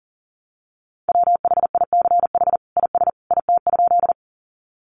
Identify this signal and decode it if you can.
Morse code, heard as a keyed tone, about 40 wpm, 700 Hz. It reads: WHICH IS IT?
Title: Sentences from Top 100 Words 40wpm